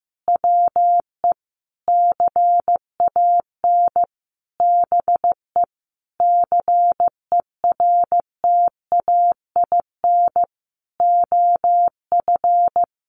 Morse code, heard as a keyed tone, about 15 wpm, 700 Hz. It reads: WE CAN BE CERTAIN OF
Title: Sentences from Top 1100 Words 15wpm